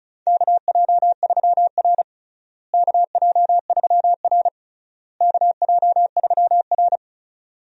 Morse code, heard as a keyed tone, about 36 wpm, 700 Hz. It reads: KJ3R KJ3R KJ3R
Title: USA Call Signs - 4 Characters - Repeated 3x - 35wpm